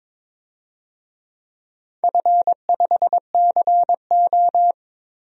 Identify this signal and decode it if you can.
Morse code, heard as a keyed tone, about 22 wpm, 700 Hz. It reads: F5CO
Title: DXCC-100 Call Signs - 4 Characters - Repeated 2x - 22wpm